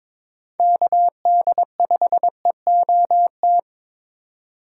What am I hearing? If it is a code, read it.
Morse code, heard as a keyed tone, about 26 wpm, 700 Hz. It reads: KD5EOT
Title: Speed Racing - USA 2x3 Call Signs 22wpm